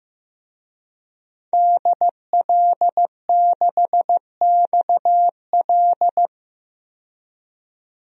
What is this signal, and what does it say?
Morse code, heard as a keyed tone, about 15 wpm, 700 Hz. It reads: DL6XL
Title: DXCC-100 Call Signs - 5 Characters - 15wpm